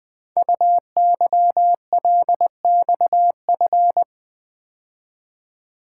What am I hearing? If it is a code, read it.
Morse code, heard as a keyed tone, about 20 wpm, 700 Hz. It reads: UYLXF
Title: Code Groups - 5 Letters - 20wpm